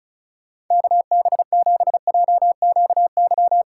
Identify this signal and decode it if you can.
Morse code, heard as a keyed tone, about 44 wpm, 700 Hz. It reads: KB7JQY